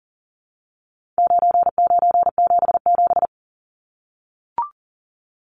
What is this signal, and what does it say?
Morse code, heard as a keyed tone, about 40 wpm, 700 Hz. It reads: 9977
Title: Numbers - 4-Digit 40wpm